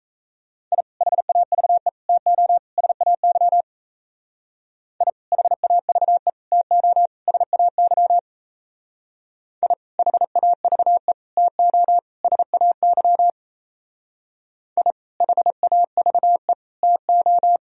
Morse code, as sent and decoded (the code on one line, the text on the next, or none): .. / .... .- ...- . / - --- / ... .- -.-- / .. / .... .- ...- . / - --- / ... .- -.-- / .. / .... .- ...- . / - --- / ... .- -.-- / .. / .... .- ...- . / - ---
I HAVE TO SAY I HAVE TO SAY I HAVE TO SAY I HAVE TO